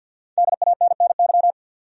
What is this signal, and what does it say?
Morse code, heard as a keyed tone, about 47 wpm, 700 Hz. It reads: DANNY
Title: USA Common Names - Max 5 Letters - 50wpm